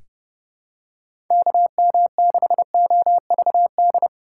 Morse code, heard as a keyed tone, about 32 wpm, 700 Hz. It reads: KM6OVD
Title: USA Call Signs - 6 Characters - 4x Character Spacing - 30wpm